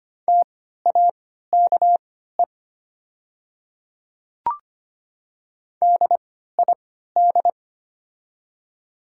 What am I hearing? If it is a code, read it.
Morse code, heard as a keyed tone, about 25 wpm, spaced as for 14 wpm, 690 Hz. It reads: TAKE DID
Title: 3x Character Spacing - Top 200 Words - 25wpm